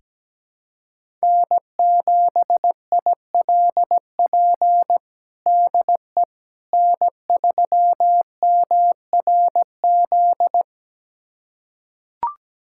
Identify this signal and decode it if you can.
Morse code, heard as a keyed tone, about 17 wpm, 700 Hz. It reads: N7ILP DE N3MRZ
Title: USA Callsign DE Callsign - 5 Characters - 17wpm